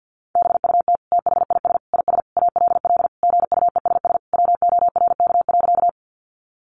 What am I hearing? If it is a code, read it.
Morse code, heard as a keyed tone, about 48 wpm, 700 Hz. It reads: BUT THIS IS ALL GUESS WORK.
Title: Sentences from Top 1500 Words 50wpm